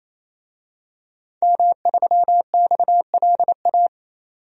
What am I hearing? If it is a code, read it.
Morse code, heard as a keyed tone, about 27 wpm, 700 Hz. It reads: M3XLA
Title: DXCC-100 Call Signs - 5 Characters - 28wpm